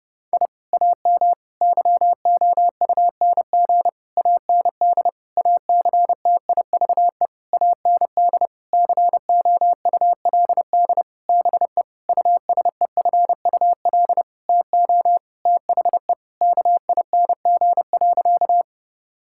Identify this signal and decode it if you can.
Morse code, heard as a keyed tone, about 30 wpm, 700 Hz. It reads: I AM YOUNG AND ACTIVE AND COULD BE USEFUL TO THE KING.
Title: Sentences from Top 1000 Words - Review Part 6 - 30wpm